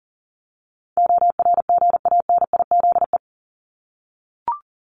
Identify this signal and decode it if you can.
Morse code, heard as a keyed tone, about 40 wpm, 700 Hz. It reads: ORGANIZE
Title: Top 1751-2000 Words 40wpm